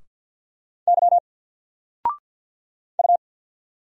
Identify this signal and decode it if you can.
Morse code, heard as a keyed tone, about 50 wpm, 700 Hz. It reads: Y U
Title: Single Letter-Number 50wpm